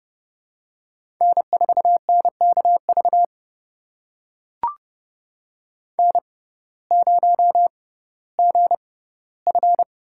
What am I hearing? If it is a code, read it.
Morse code, heard as a keyed tone, about 30 wpm, spaced as for 17 wpm, 700 Hz. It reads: N4NKV N0GF
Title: USA Call Signs - 5 Characters - 6x Character Spacing - 30wpm